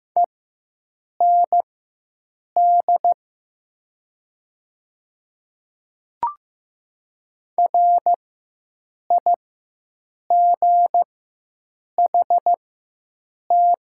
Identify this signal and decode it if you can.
Morse code, heard as a keyed tone, about 15 wpm, spaced as for 7 wpm, 700 Hz. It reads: END RIGHT
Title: 4x Character Spacing - Top 200 Words - 15wpm